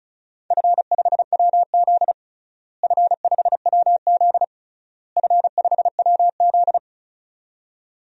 Morse code, heard as a keyed tone, about 34 wpm, 700 Hz. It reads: F5WZ F5WZ F5WZ
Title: DXCC-100 Call Signs - 4 Characters - Repeated 3x - 35wpm